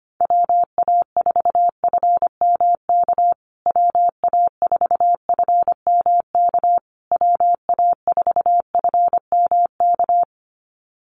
Morse code, heard as a keyed tone, about 25 wpm, 700 Hz. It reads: WA4FMK WA4FMK WA4FMK
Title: USA Call Signs Repeated 3x 25wpm